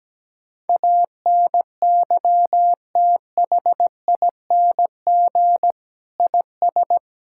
Morse code, heard as a keyed tone, about 17 wpm, 700 Hz. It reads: ANYTHING IS
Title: Sentences from Top 1000 Words - Review Part 6 - 17wpm